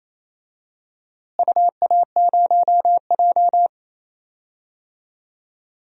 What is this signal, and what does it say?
Morse code, heard as a keyed tone, about 27 wpm, 700 Hz. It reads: UA0J
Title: DXCC-100 Call Signs - 4 Characters - Repeated 3x - 28wpm